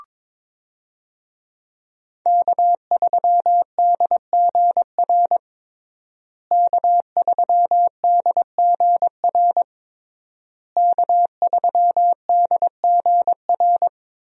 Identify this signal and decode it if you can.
Morse code, heard as a keyed tone, about 22 wpm, 700 Hz. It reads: K3DGR K3DGR K3DGR
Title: USA Call Signs - 5 Characters - Repeated 3x - 22wpm